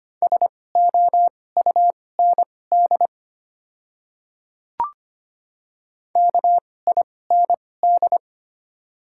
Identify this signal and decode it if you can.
Morse code, heard as a keyed tone, about 25 wpm, spaced as for 18 wpm, 700 Hz. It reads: SOUND KIND
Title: 2x Character Spacing - Top 200 Words - 25wpm